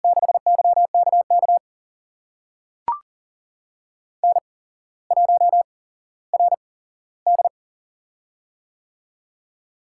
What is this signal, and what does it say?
Morse code, heard as a keyed tone, about 40 wpm, spaced as for 17 wpm, 700 Hz. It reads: 6YKK N1RD
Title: USA Call Signs - 8x Character Spacing - 40wpm